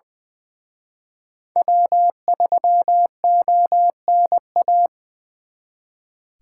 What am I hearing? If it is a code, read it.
Morse code, heard as a keyed tone, about 25 wpm, 700 Hz. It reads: W3ONA
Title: Speed Racing - USA 1x3 Call Signs 20wpm